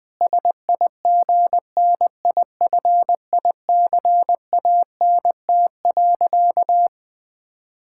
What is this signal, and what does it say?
Morse code, heard as a keyed tone, about 20 wpm, 700 Hz. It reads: SIGNIFICANT.
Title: Sentences from Top 1000 Words - Review Part 6 - 20wpm